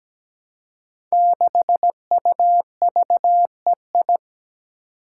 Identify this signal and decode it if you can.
Morse code, heard as a keyed tone, about 17 wpm, 700 Hz. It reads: 6UVEI
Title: Five Challenging Letter-Number Combinations 17wpm